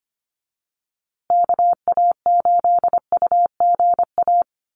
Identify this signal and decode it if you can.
Morse code, heard as a keyed tone, about 25 wpm, 700 Hz. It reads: KA8UGA
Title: USA Call Signs Repeated 2x 25wpm